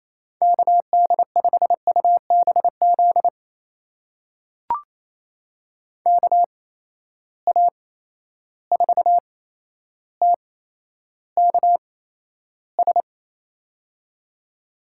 Morse code, heard as a keyed tone, about 28 wpm, spaced as for 12 wpm, 700 Hz. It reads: KD5UBZ KA4TKS
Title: USA Call Signs - 6 Characters - 8x Character Spacing - 28wpm